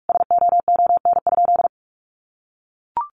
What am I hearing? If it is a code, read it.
Morse code, heard as a keyed tone, about 44 wpm, 700 Hz. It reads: SOON?
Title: Sentences from Top 400 Words 45wpm